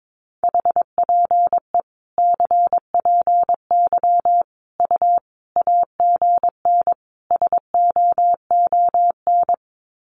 Morse code, heard as a keyed tone, about 22 wpm, 700 Hz. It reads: HPE CPY U AGN SOON
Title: Common QSO Phrases 22wpm